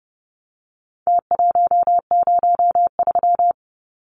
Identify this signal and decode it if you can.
Morse code, heard as a keyed tone, about 30 wpm, 700 Hz. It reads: T103
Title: Numbers - Contest Style - 4-Digit 30wpm